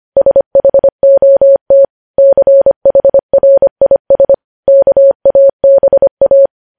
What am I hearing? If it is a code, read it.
Morse code, heard as a keyed tone, about 25 wpm, 550 Hz. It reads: SHOT CHRIS KABA
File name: podcast / News headlines-1x-25x25